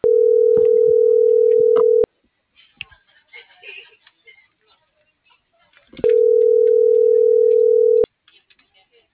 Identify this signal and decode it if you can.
Ringback tone (North America), 440 + 480 Hz.